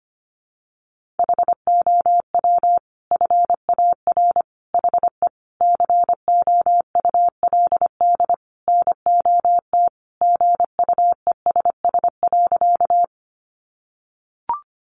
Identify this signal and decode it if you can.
Morse code, heard as a keyed tone, about 25 wpm, 700 Hz. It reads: HOW FAR HE COULD NOT GUESS.
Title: Sentences from Top 1500 Words 25wpm